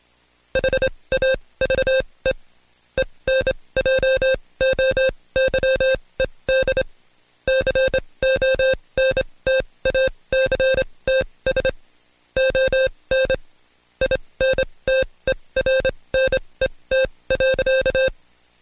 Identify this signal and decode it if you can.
Morse code, heard as a keyed tone, about 26 wpm, spaced as for 20 wpm, 530 Hz. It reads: HAVE ENJOYED CONTACTS ON INTERNET.
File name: Code Quick / AudioDownld2 / Seg136-Stepper9 19WPM